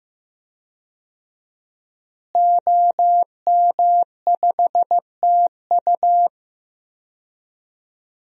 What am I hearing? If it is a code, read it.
Morse code, heard as a keyed tone, about 15 wpm, 700 Hz. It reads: OM5TU
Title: DXCC-100 Call Signs - 5 Characters - Repeated 3x - 15wpm